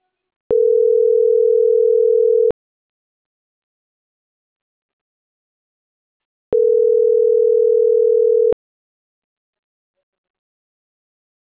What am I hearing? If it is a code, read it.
Ringback tone (North America), 440 + 480 Hz.
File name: CENTRAL MAERSK / out adrian.moreno